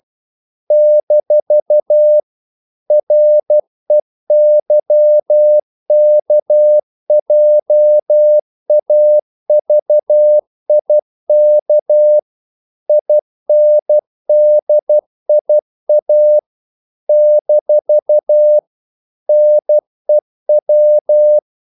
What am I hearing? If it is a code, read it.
Morse code, heard as a keyed tone, about 12 wpm, 600 Hz. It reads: - REYKJAVIK INDIA - NEW
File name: CW / Practice / NationalCapitals12